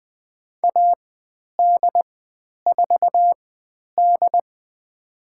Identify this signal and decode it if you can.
Morse code, heard as a keyed tone, about 20 wpm, spaced as for 10 wpm, 700 Hz. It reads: AD4D
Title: 102 Lesson 17g - Letter u - Call Signs 20wpm